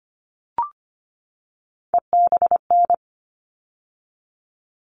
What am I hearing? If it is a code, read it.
Morse code, heard as a keyed tone, about 25 wpm, 700 Hz. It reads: EBN
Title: All Cut Numbers - 3-Digit 25wpm